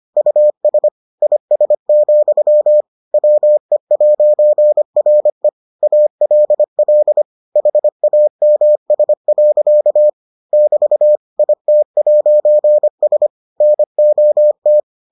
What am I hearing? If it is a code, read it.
Morse code, heard as a keyed tone, about 25 wpm, 600 Hz. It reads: US IS, WE'RE ALL HAMS. <BT> IT'S NOT